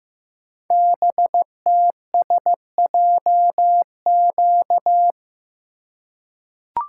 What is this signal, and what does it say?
Morse code, heard as a keyed tone, about 15 wpm, 700 Hz. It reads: BTSJQ
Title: Code Groups - 5 Letters - 15wpm